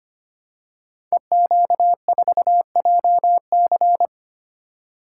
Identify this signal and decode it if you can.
Morse code, heard as a keyed tone, about 25 wpm, 700 Hz. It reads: EQ4JC
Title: Code Groups - 5 Letter:Numbers - 25wpm